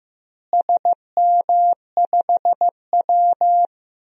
Morse code, heard as a keyed tone, about 15 wpm, 700 Hz. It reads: SM5W
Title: DXCC-100 Call Signs - 4 Characters - Repeated 3x - 15wpm